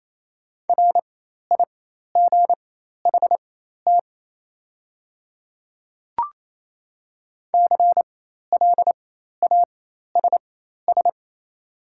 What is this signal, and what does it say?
Morse code, heard as a keyed tone, about 28 wpm, spaced as for 13 wpm, 700 Hz. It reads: RIGHT CLASS
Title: 4x Character Spacing - Top 500 Words - 28wpm